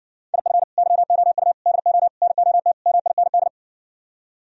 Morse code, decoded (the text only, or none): IF YOU DO NOT MIND